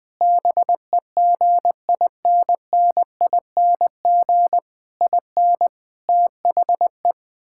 Morse code, heard as a keyed tone, about 20 wpm, 700 Hz. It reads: BEGINNING IN THE
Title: Sentences from Top 1000 Words - Review Part 6 - 20wpm